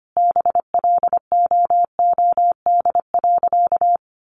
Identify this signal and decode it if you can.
Morse code, heard as a keyed tone, about 25 wpm, 700 Hz. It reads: BLOOD.
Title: A Sound Of Thunder 25wpm